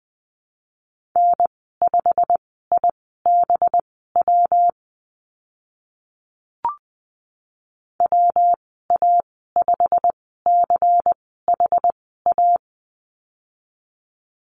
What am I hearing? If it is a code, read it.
Morse code, heard as a keyed tone, about 20 wpm, spaced as for 15 wpm, 700 Hz. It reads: N5IBW WA5CHA
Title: 200 Lesson 32g - Letter k - Call Signs 20wpm